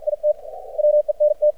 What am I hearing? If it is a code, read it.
Morse code, heard as a keyed tone, about 50 wpm, 600 Hz. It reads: IT WENT
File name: archived / CRSnet